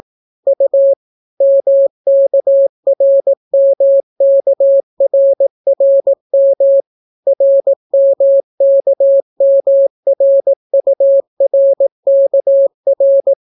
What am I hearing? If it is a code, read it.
Morse code, heard as a keyed tone, about 18 wpm, 550 Hz. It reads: U MKRMKRRM RMKMRURKR